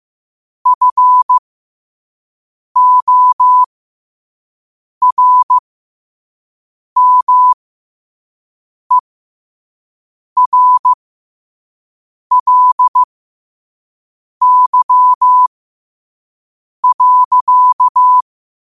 Morse code, heard as a keyed tone, about 15 wpm, spaced as for 5 wpm, 1000 Hz. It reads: FORMERLY.
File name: qotdfiles / 5wpm